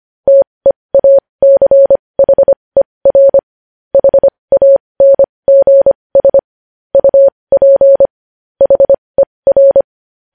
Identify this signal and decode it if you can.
Morse code, heard as a keyed tone, about 25 wpm, spaced as for 20 wpm, 550 Hz. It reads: TEACHER HANGS UP HER
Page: News Headlines in Morse Code at 20 WPM